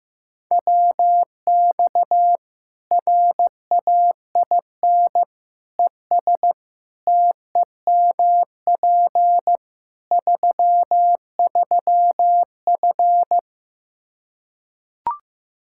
Morse code, heard as a keyed tone, about 15 wpm, 700 Hz. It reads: WX RAIN ES TEMP 33F
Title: Practice QSO - Exchange 2 - Info - 15wpm